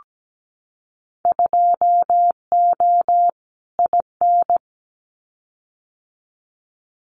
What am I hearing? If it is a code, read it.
Morse code, heard as a keyed tone, about 17 wpm, 700 Hz. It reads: 2O IN
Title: Winter Field Day Exchanges 17wpm